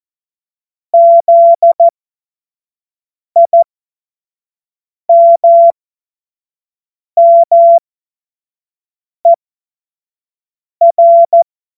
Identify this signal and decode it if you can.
Morse code, heard as a keyed tone, about 14 wpm, spaced as for 5 wpm, 680 Hz. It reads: ZIMMER